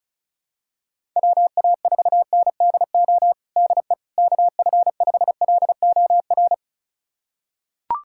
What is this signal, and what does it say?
Morse code, heard as a keyed tone, about 36 wpm, 700 Hz. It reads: WA4NDO DE KF5LOR
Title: USA Callsign DE Callsign 35wpm